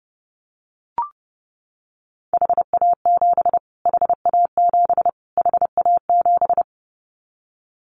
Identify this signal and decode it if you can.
Morse code, heard as a keyed tone, about 30 wpm, 700 Hz. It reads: HA7 HA7 HA7
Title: Callsign Prefixes for Top 25 DXCC - Repeated 3x - 30wpm